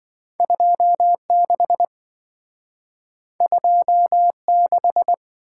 Morse code, heard as a keyed tone, about 27 wpm, 700 Hz. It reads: 26 26